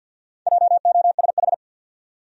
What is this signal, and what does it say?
Morse code, heard as a keyed tone, about 52 wpm, 700 Hz. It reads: JOSH
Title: USA Common Names - Max 4 Letters - 50wpm